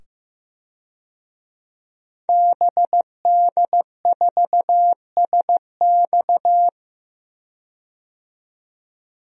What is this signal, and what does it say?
Morse code, heard as a keyed tone, about 15 wpm, 700 Hz. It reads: BD4SX
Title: DXCC-100 Call Signs - 5 Characters - 15wpm